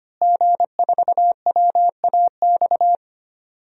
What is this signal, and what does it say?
Morse code, heard as a keyed tone, about 25 wpm, 700 Hz. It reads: G4WAX
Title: DXCC-100 Call Signs - 5 Characters - Repeated 3x - 25wpm